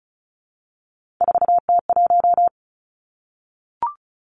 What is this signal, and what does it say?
Morse code, heard as a keyed tone, about 34 wpm, 700 Hz. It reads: 4T1
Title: Three Challenging Letter-Number Combinations 35wpm